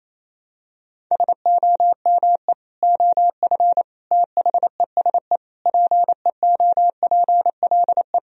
Morse code, heard as a keyed tone, about 28 wpm, 700 Hz. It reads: SOME OF THESE PEOPLE
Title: Sentences from Top 1000 Words - Review Part 3 - 28wpm